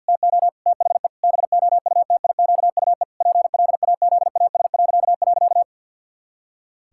Morse code, heard as a keyed tone, about 49 wpm, 700 Hz. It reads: TO THE BOUTIQUE PLAZAS..
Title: Pangram Challenge v2 - 50wpm